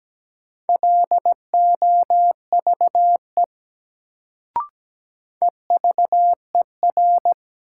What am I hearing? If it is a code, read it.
Morse code, heard as a keyed tone, about 17 wpm, 700 Hz. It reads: LOVE EVER
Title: Sentences from Top 400 Words - New Word Drill - 17wpm